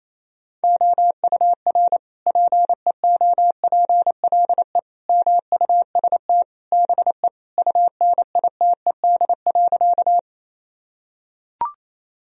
Morse code, heard as a keyed tone, about 27 wpm, 700 Hz. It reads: OUR PEOPLE MUST BE UNITED.
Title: Sentences from Top 1000 Words - Review Part 4 - 28wpm